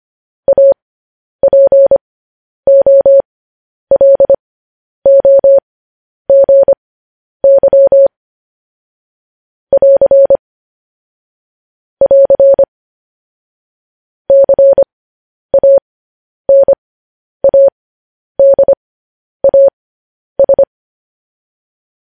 Morse code, heard as a keyed tone, about 25 wpm, spaced as for 10 wpm, 550 Hz. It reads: APOLOGY <AR> <AR> CANADAS
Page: news headlines in morse code at 10 wpm